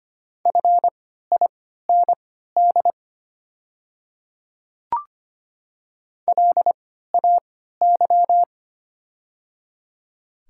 Morse code, heard as a keyed tone, about 25 wpm, spaced as for 14 wpm, 700 Hz. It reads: FIND LAY
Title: 3x Character Spacing - Top 500 Words - 25wpm